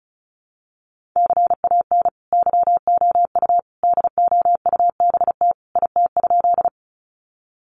Morse code, heard as a keyed tone, about 34 wpm, 700 Hz. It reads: CAN YOU DOUBT IT?
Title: Sentences from Top 1500 Words 35wpm